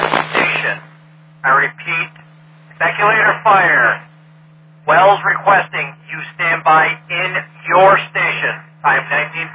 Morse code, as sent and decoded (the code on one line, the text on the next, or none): none